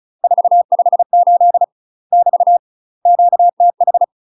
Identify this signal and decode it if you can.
Morse code, heard as a keyed tone, about 36 wpm, 700 Hz. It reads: 458 <BT> QTH
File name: UK MASTER QSOs TRAINING / 35WPM / UK MASTER QSOs 0029